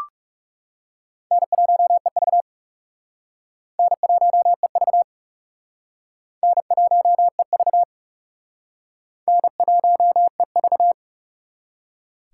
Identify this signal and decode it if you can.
Morse code, heard as a keyed tone, about 37 wpm, 700 Hz. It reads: N1EV N1EV N1EV N1EV
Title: Speed Racing - USA 1x2 Call Signs 30wpm